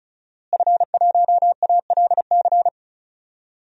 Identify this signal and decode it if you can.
Morse code, heard as a keyed tone, about 36 wpm, 700 Hz. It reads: F1ALC
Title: DXCC-100 Call Signs - 5 Characters - Repeated 3x - 35wpm